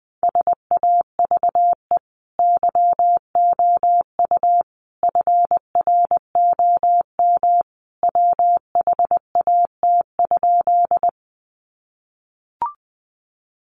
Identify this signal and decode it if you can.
Morse code, heard as a keyed tone, about 20 wpm, 700 Hz. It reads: SAVE YOU FROM WHAT?
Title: Sentences from Top 1100 Words 20wpm